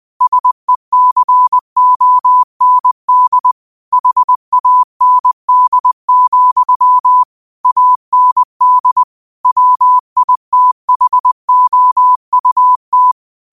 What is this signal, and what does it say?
Morse code, heard as a keyed tone, about 20 wpm, 1000 Hz. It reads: SECOND HAND, AND WITHOUT
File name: qotdfiles / 20wpm